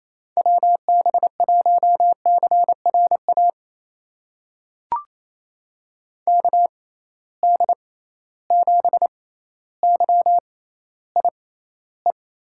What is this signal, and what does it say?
Morse code, heard as a keyed tone, about 28 wpm, spaced as for 14 wpm, 700 Hz. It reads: WB1CRA KD7YIE
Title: USA Call Signs - 6 Characters - 6x Character Spacing - 28wpm